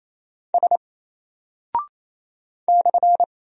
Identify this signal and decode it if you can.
Morse code, heard as a keyed tone, about 28 wpm, 700 Hz. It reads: S /
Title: Single Character (Letter:Number:Punctuation:Prosign) 28wpm